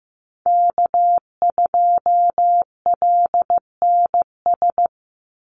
Morse code, heard as a keyed tone, about 15 wpm, 700 Hz. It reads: K2LNS
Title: Call Signs Repeated 3x - 15wpm